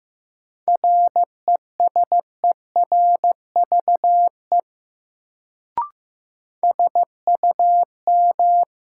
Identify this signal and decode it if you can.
Morse code, heard as a keyed tone, about 15 wpm, 700 Hz. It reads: RESERVE SUM
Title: Sentences from Top 1000 Words - Review Part 6 - New Word Drill - 15wpm